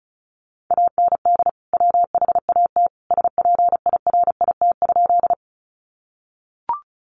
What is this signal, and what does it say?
Morse code, heard as a keyed tone, about 34 wpm, 700 Hz. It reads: AND WHAT SPIRIT?
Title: Sentences from Top 1400 Words 35wpm